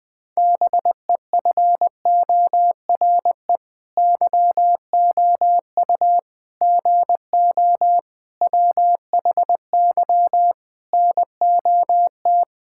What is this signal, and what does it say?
Morse code, heard as a keyed tone, about 20 wpm, 700 Hz. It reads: BEFORE YOU GO WHY NOT
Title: Sentences with 9-Word Phrases - 20wpm